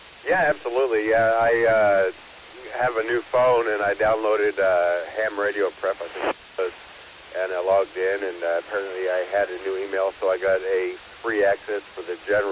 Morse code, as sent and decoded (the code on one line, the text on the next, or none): none